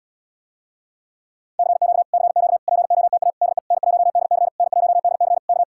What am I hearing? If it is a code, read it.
Morse code, heard as a keyed tone, about 77 wpm, 700 Hz. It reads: CQ CQ POTA DE N1DC N1DC K